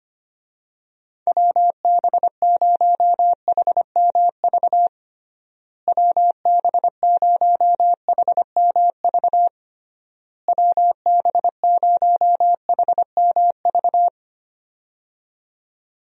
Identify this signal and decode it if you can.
Morse code, heard as a keyed tone, about 25 wpm, 700 Hz. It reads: WB0HMV WB0HMV WB0HMV
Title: USA Call Signs - 6 Characters - Repeated 3x - 25wpm